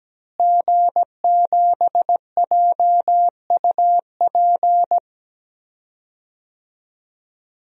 Morse code, heard as a keyed tone, about 17 wpm, 700 Hz. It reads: G7JUP
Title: DXCC-100 Call Signs - 5 Characters - Repeated 2x - 17wpm